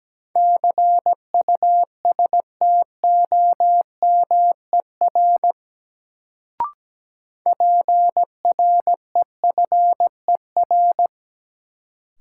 Morse code, heard as a keyed tone, about 17 wpm, 700 Hz. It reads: CUSTOMER PREFER
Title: Sentences from Top 1100 Words - New Word Drill Alt - 17wpm